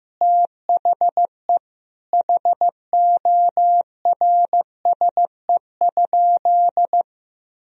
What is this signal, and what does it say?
Morse code, heard as a keyed tone, about 16 wpm, 700 Hz. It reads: THE HORSE?
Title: Sentences from Top 500 Words and Abbreviations 15wpm